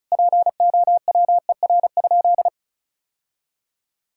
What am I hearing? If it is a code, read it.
Morse code, heard as a keyed tone, about 34 wpm, 700 Hz. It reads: POWER?
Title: Sentences from Top 1000 Words - Review Part 5 - 35wpm